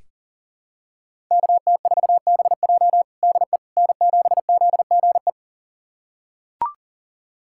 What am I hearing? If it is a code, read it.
Morse code, heard as a keyed tone, about 37 wpm, 700 Hz. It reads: KT4BJ DE N7ZGE
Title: USA Callsign DE Callsign - 5 Characters - 40wpm